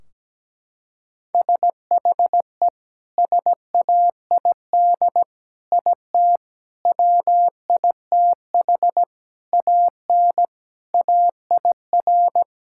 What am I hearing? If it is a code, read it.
Morse code, heard as a keyed tone, about 17 wpm, 700 Hz. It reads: SHE SAID IT WITH AN AIR
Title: Sentences from Top 1600 Words 17wpm